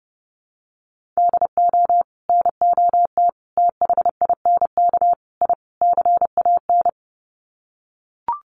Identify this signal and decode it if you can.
Morse code, heard as a keyed tone, about 30 wpm, 700 Hz. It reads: DO NOT THINK I CAN
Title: Sets of 5 Words Encore2 30wpm